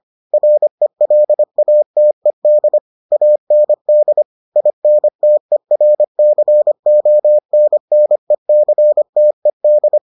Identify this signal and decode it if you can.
Morse code, heard as a keyed tone, about 25 wpm, 600 Hz. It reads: RELATED AND INTERCONNECTED